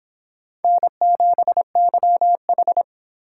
Morse code, heard as a keyed tone, about 25 wpm, 700 Hz. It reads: N7YH